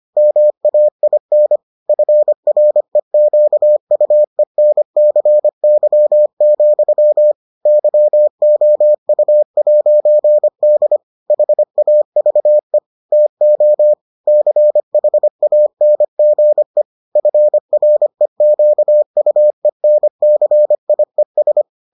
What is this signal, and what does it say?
Morse code, heard as a keyed tone, about 25 wpm, 600 Hz. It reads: MAIN FREQUENCY, YOU'D HAVE TO CHANGE FREQUENCIES